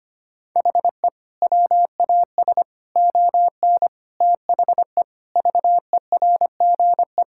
Morse code, heard as a keyed tone, about 25 wpm, 700 Hz. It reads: HE WAS ON THE VERGE
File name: practice / 25wpm / Sentences with 4-Word Phrases - 25wpm